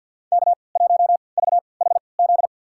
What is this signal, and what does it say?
Morse code, heard as a keyed tone, about 53 wpm, spaced as for 29 wpm, 700 Hz. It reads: K1VHZ
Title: USA Call Signs - 3x Character Spacing - 50wpm